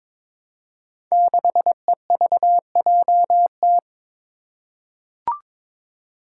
Morse code, heard as a keyed tone, about 22 wpm, 700 Hz. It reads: 6EVJT
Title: Five Challenging Letter-Number Combinations 22wpm